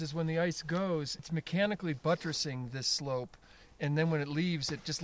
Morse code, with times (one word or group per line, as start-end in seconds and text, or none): none